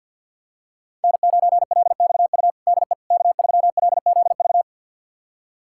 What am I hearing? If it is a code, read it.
Morse code, heard as a keyed tone, about 51 wpm, 700 Hz. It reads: N9LXU DE K3LZV
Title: USA Callsign DE Callsign - 5 Characters - 50wpm